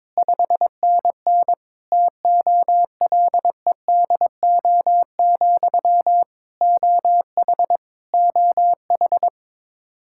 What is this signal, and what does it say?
Morse code, heard as a keyed tone, about 22 wpm, 700 Hz. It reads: 5NN TOLEDO, OH OH
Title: Practice QSO - Exchange 1 - Beginning - 22wpm